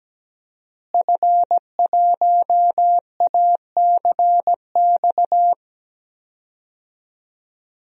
Morse code, heard as a keyed tone, about 17 wpm, 700 Hz. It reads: F1ACX
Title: DXCC-100 Call Signs - 5 Characters - Repeated 3x - 17wpm